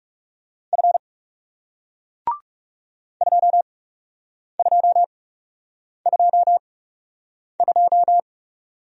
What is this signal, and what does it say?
Morse code, heard as a keyed tone, about 38 wpm, spaced as for 8 wpm, 700 Hz. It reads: F 2222
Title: Speed-Racing Single Letter-Number 30wpm